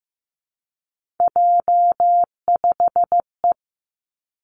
Morse code, heard as a keyed tone, about 15 wpm, 700 Hz. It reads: J5E
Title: Three Challenging Letter-Number Combinations 15wpm